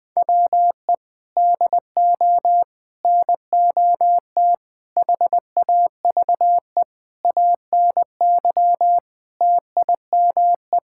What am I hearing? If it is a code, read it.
Morse code, heard as a keyed tone, about 20 wpm, 700 Hz. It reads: WE DO NOT HAVE ANY TIME
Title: Sentences with 4-Word Phrases - 20wpm